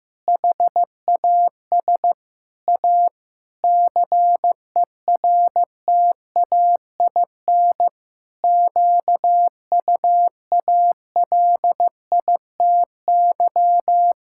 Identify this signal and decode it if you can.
Morse code, heard as a keyed tone, about 15 wpm, 700 Hz. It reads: HAS A CERTAIN QUALITY
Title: Sentences from Top 1000 Words - Review Part 6 - 15wpm